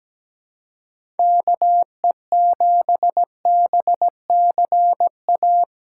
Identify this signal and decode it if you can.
Morse code, heard as a keyed tone, about 17 wpm, 700 Hz. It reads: KE7BCA
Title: USA Callsign DE Callsign 17wpm